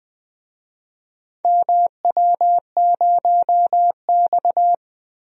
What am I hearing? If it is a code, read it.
Morse code, heard as a keyed tone, about 20 wpm, 700 Hz. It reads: MW0X
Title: DXCC-100 Call Signs - 4 Characters - Repeated 2x - 20wpm